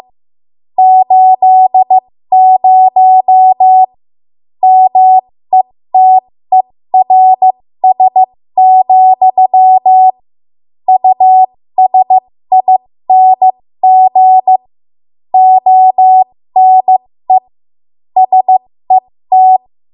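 Morse code, heard as a keyed tone, about 15 wpm, spaced as for 13 wpm, 750 Hz. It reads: 80 METERS, USING ONE SET
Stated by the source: 13 wpm Morse code practice archive transmission.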